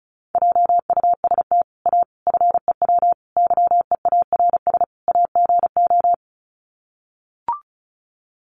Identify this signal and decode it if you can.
Morse code, heard as a keyed tone, about 34 wpm, 700 Hz. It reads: JUST A FEW YEARS AGO
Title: Sets of 5 Words Encore 35wpm